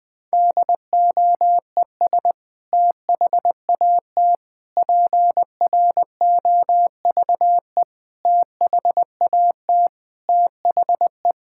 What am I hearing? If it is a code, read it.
Morse code, heard as a keyed tone, about 20 wpm, 700 Hz. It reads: DOES THAT PROVE THAT THE
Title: Sentences from Top 1600 Words 20wpm